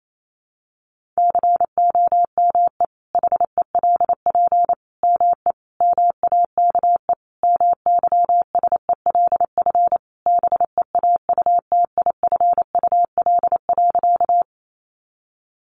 Morse code, heard as a keyed tone, about 28 wpm, 700 Hz. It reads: COME HELP ME MAKE MYSELF BEAUTIFUL.
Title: Sentences from Top 1300 Words 28wpm